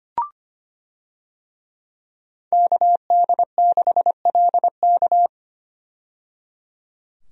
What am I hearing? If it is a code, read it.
Morse code, heard as a keyed tone, about 25 wpm, 700 Hz. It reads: KD6LK
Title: USA Call Signs - 5 Characters - 25wpm